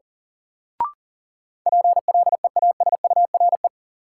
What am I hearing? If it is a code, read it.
Morse code, heard as a keyed tone, about 37 wpm, 700 Hz. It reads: PLEASURE
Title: Sentences from Top 1600 Words - New Word Drill - 40wpm